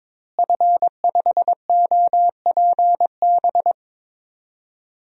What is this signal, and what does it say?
Morse code, heard as a keyed tone, about 22 wpm, 700 Hz. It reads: F5OPB
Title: DXCC-100 Call Signs - 5 Characters - Repeated 2x - 22wpm